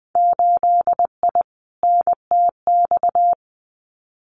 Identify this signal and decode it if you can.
Morse code, heard as a keyed tone, about 19 wpm, 700 Hz. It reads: 8I NTX
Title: Winter Field Day Exchanges Encore 20wpm